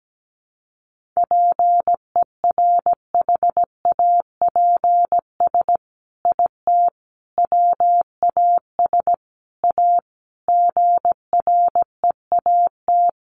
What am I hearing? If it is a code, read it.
Morse code, heard as a keyed tone, about 17 wpm, 700 Hz. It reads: PERHAPS IT WAS A GREAT
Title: Sentences from Top 1400 Words 17wpm